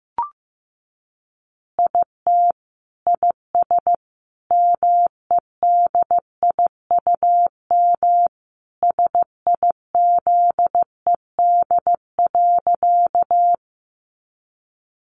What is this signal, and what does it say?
Morse code, heard as a keyed tone, about 15 wpm, 700 Hz. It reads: IT IS MEDIUM SIZED.
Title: Easy Phrases 15wpm